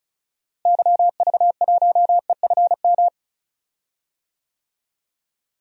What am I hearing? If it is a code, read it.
Morse code, heard as a keyed tone, about 36 wpm, 700 Hz. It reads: YV1EFM
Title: DXCC-100 Call Signs - 6 Characters - Repeated 3x - 35wpm